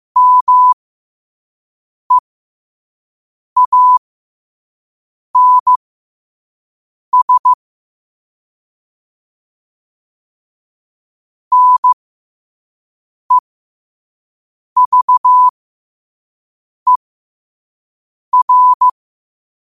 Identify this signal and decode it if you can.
Morse code, heard as a keyed tone, about 15 wpm, spaced as for 5 wpm, 1000 Hz. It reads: MEANS NEVER
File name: qotdfiles / 5wpm